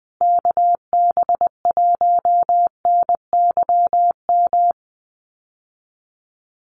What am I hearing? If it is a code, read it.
Morse code, heard as a keyed tone, about 20 wpm, 700 Hz. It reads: KB1NYM
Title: USA Call Signs Repeated 2x 20wpm